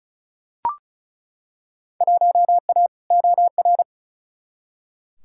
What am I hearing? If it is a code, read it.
Morse code, heard as a keyed tone, about 36 wpm, 700 Hz. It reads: 1A OR
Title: Field Day Exchanges - 35wpm